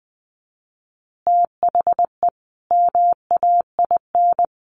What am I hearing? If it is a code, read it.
Morse code, heard as a keyed tone, about 20 wpm, 700 Hz. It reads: THE MAIN
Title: Additional Easy Phrases 20wpm